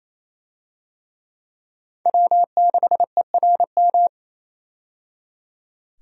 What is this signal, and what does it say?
Morse code, heard as a keyed tone, about 28 wpm, 700 Hz. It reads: W6ERM
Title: USA Call Signs - 5 Characters - 28wpm